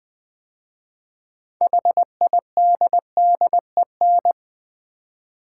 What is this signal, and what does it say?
Morse code, heard as a keyed tone, about 20 wpm, 700 Hz. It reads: HIDDEN
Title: 6 Letter Words - First 3 Letters Given - 20wpm